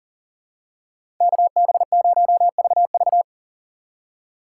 Morse code, heard as a keyed tone, about 40 wpm, 700 Hz. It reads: KB0VV
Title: USA Call Signs - 5 Characters - 6x Character Spacing - 40wpm